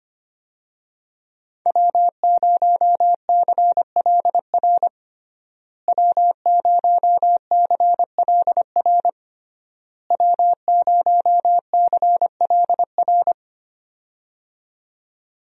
Morse code, heard as a keyed tone, about 25 wpm, 700 Hz. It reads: W0CLR W0CLR W0CLR
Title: USA Call Signs - 5 Characters - Repeated 3x - 25wpm